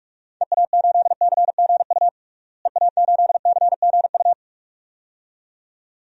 Morse code, heard as a keyed tone, about 42 wpm, 700 Hz. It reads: EA8CGU EA8CGU
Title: DXCC-100 Call Signs - 6 Characters - Repeated 2x - 45wpm